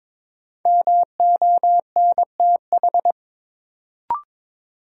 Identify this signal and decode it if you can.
Morse code, heard as a keyed tone, about 22 wpm, 700 Hz. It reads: MONTH
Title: Sentences from Top 800 Words - New Word Drill - 22wpm